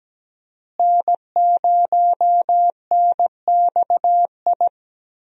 Morse code, heard as a keyed tone, about 17 wpm, 700 Hz. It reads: N0NXI
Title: USA Call Signs - 5 Characters - Repeated 3x - 17wpm